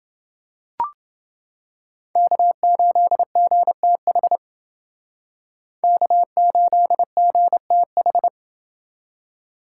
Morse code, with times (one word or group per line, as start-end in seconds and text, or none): 2.2-4.4 K8GTH
5.8-8.3 K8GTH